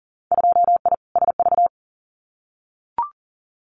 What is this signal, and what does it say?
Morse code, heard as a keyed tone, about 41 wpm, 700 Hz. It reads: 2I SV
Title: Winter Field Day Exchanges Encore 40wpm